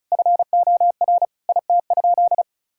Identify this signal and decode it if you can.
Morse code, heard as a keyed tone, about 36 wpm, 700 Hz. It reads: FOR IT?
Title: Sentences from Top 1000 Words - Review Part 6 - 35wpm